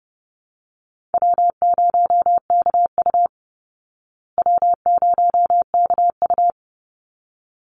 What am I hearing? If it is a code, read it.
Morse code, heard as a keyed tone, about 30 wpm, 700 Hz. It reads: W0KU W0KU
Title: Easy Call Signs Repeated 2x - 30wpm